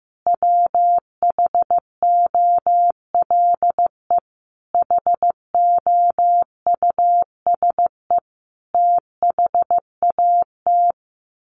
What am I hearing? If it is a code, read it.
Morse code, heard as a keyed tone, about 15 wpm, 700 Hz. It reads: WHOLE HOUSE THAT
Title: Sentences from Top 1200 Words 15wpm